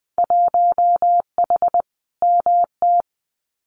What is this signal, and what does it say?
Morse code, heard as a keyed tone, about 19 wpm, 700 Hz. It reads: 1H MT
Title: Winter Field Day Exchanges Encore 20wpm